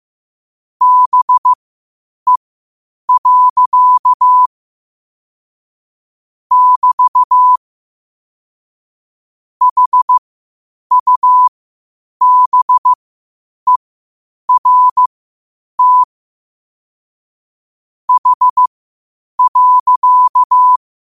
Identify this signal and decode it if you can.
Morse code, heard as a keyed tone, about 15 wpm, spaced as for 8 wpm, 1000 Hz. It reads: BE. <BT> HUBERT H.
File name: qotdfiles / 8wpm